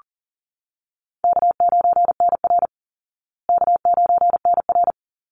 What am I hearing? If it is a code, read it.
Morse code, heard as a keyed tone, about 40 wpm, 700 Hz. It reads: K9NR K9NR
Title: Easy Call Signs Repeated 2x - 40wpm